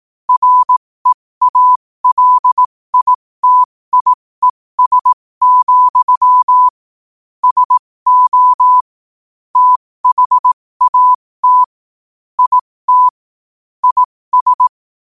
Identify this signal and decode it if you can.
Morse code, heard as a keyed tone, about 18 wpm, spaced as for 15 wpm, 1000 Hz. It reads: REALITIES, SO THAT IT IS
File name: qotdfiles / 15wpm